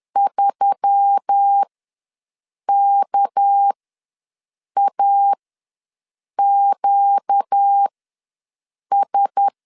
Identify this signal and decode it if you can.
Morse code, heard as a keyed tone, about 10 wpm, spaced as for 6 wpm, 800 Hz. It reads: 3KAQS
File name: LEZIONI / 6 PKX0 / 6 PK ws 10wpm 3745